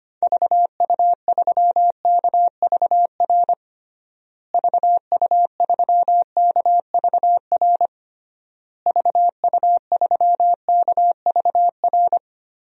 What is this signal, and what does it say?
Morse code, heard as a keyed tone, about 25 wpm, 700 Hz. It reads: VU3KVR VU3KVR VU3KVR
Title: DXCC-100 Call Signs - 6 Characters - Repeated 3x - 25wpm